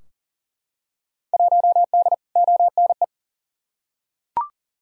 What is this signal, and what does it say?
Morse code, heard as a keyed tone, about 40 wpm, 700 Hz. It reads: 1D ONE
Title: Field Day Exchanges - 40wpm